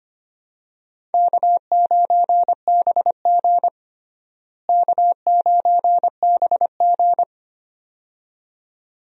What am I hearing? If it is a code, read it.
Morse code, heard as a keyed tone, about 25 wpm, 700 Hz. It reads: K9BG K9BG
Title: USA Call Signs - 4 Characters - Repeated 2x - 25wpm